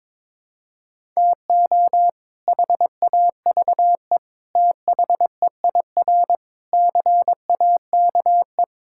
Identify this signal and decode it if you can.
Morse code, heard as a keyed tone, about 22 wpm, 700 Hz. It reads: TO HAVE THEIR CAKE
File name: practice / 22wpm / Sets of 8 Words - 22wpm